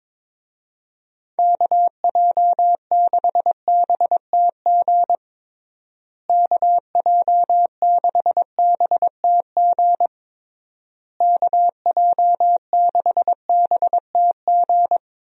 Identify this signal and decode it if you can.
Morse code, heard as a keyed tone, about 22 wpm, 700 Hz. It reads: KJ6BTG KJ6BTG KJ6BTG
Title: USA Call Signs - 6 Characters - Repeated 3x - 22wpm